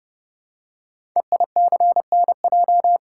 Morse code, heard as a keyed tone, about 28 wpm, 700 Hz. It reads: EICNJ
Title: Code Groups - 5 Letters - 30wpm